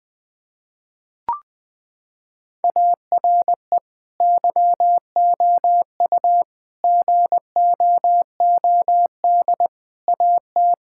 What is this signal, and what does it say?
Morse code, heard as a keyed tone, about 20 wpm, 700 Hz. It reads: ARE YOU GOOD AT
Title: Sentences from Top 1000 Words - Review Part 6 - 20wpm